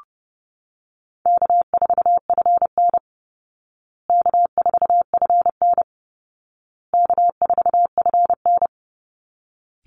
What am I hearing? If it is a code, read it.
Morse code, heard as a keyed tone, about 30 wpm, 700 Hz. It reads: K4FN K4FN K4FN
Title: Call Signs Repeated 2x - 30wpm